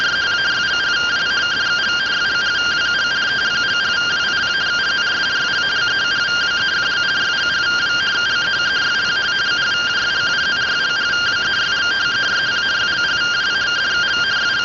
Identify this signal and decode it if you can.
Two-tone teleprinter signal, about 170 Hz shift.